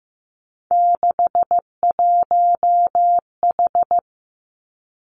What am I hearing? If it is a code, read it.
Morse code, heard as a keyed tone, about 15 wpm, 700 Hz. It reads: 61H
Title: Three Challenging Letter-Number Combinations 15wpm